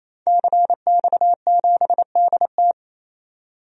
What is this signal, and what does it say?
Morse code, heard as a keyed tone, about 28 wpm, 700 Hz. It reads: CX7DT
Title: DXCC-100 Call Signs - 5 Characters - 28wpm